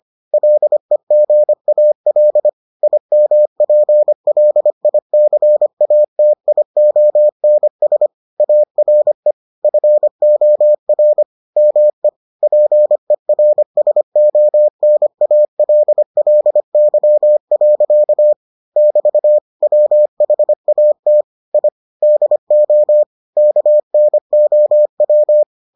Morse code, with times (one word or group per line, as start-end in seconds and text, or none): 0.3-2.5 LEGAL
2.8-8.1 IMPLICATIONS
8.4-9.3 ARE
9.6-11.2 FOR
11.6-12.1 ME
12.4-18.3 PERSONALLY.
18.8-19.4 <BT>
19.6-21.2 WHAT
21.5-21.7 I
22.0-23.0 DO
23.4-25.4 KNOW